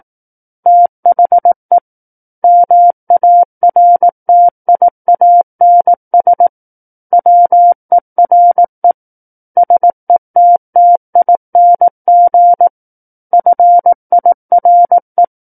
Morse code, heard as a keyed tone, about 18 wpm, 700 Hz. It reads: THE MARTIANS WERE SETTING FIRE
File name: Book One / 09-Chapter 9 - 18 WPM